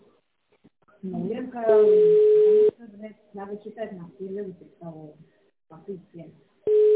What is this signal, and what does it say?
Ringback tone (Europe, 425 Hz), with speech.